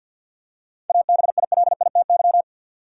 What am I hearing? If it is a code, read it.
Morse code, heard as a keyed tone, about 53 wpm, 700 Hz. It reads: ABILITY
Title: Seven Letter Words - 50wpm